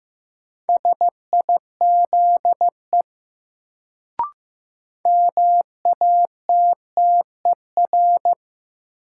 Morse code, heard as a keyed tone, about 15 wpm, 700 Hz. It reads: SIZE MATTER
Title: Sets of 3 Words - New Word Drill - 15wpm